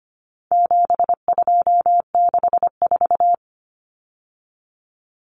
Morse code, heard as a keyed tone, about 25 wpm, 700 Hz. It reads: 7264
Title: Numbers - 4-Digit 25wpm